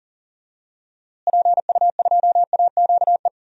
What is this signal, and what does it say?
Morse code, heard as a keyed tone, about 40 wpm, 700 Hz. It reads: PU2AQE
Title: DXCC-100 Call Signs - 6 Characters - Repeated 2x - 40wpm